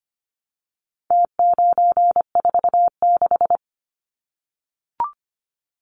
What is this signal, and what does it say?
Morse code, heard as a keyed tone, about 25 wpm, 700 Hz. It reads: T946
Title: Numbers - Contest Style - 4-Digit 25wpm